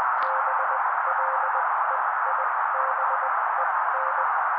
Morse code, heard as a keyed tone, about 20 wpm, spaced as for 16 wpm, 530 Hz. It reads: BLEIBEN
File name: daily / 16wpm noise